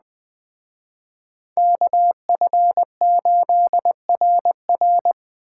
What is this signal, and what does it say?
Morse code, heard as a keyed tone, about 21 wpm, 700 Hz. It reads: KF8RR
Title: Speed Racing - USA 2x2 Call Signs 17wpm